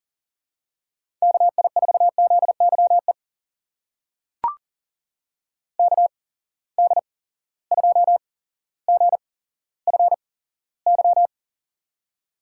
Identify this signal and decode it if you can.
Morse code, heard as a keyed tone, about 41 wpm, spaced as for 17 wpm, 700 Hz. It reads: KI4ZYE KD2GFY
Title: USA Call Signs - 8x Character Spacing - 40wpm